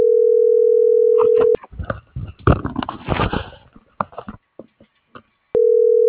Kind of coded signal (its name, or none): telephone call progress tone